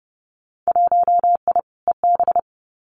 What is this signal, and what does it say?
Morse code, heard as a keyed tone, about 30 wpm, 700 Hz. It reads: 1I EB
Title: Winter Field Day Exchanges Encore 30wpm